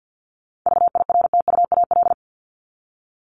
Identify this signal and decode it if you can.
Morse code, heard as a keyed tone, about 50 wpm, 700 Hz. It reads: VIRTUAL
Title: LY Suffix - Top 100 Words with Base Words 50wpm